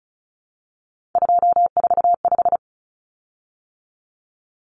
Morse code, heard as a keyed tone, about 34 wpm, 700 Hz. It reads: 245
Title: Common Cut Numbers - 3-Digit 35wpm